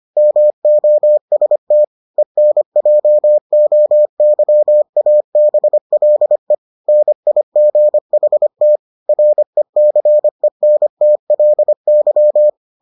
Morse code, heard as a keyed tone, about 25 wpm, 600 Hz. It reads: MOST ENJOYABLE NIGHT RECENTLY